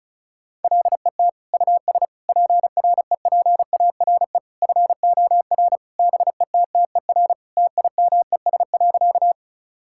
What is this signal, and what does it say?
Morse code, heard as a keyed tone, about 36 wpm, 700 Hz. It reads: LET US PREPARE FOR BETTER TIMES.
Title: Sentences from Top 1000 Words - Review Part 6 - 35wpm